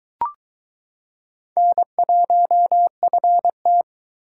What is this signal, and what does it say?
Morse code, heard as a keyed tone, about 22 wpm, 700 Hz. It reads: N1FT